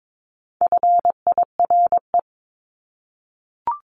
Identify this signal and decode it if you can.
Morse code, heard as a keyed tone, about 22 wpm, 700 Hz. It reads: FIRE
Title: Top 500 Words 22wpm